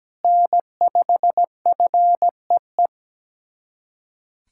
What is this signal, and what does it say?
Morse code, heard as a keyed tone, about 17 wpm, 700 Hz. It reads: N5FEE
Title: USA Callsign DE Callsign 17wpm